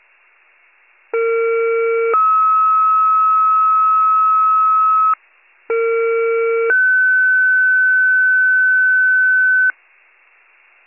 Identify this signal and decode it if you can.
Two-tone page, heard as a one-second tone, then a three-second tone, about 460 then 1290 Hz, about 460 then 1500 Hz. It is heard from 1.1 to 9.7 s.